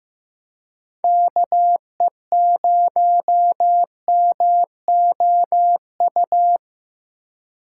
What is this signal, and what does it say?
Morse code, heard as a keyed tone, about 15 wpm, 700 Hz. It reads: KE0MOU
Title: USA Call Signs - 3x Character Spacing - 15wpm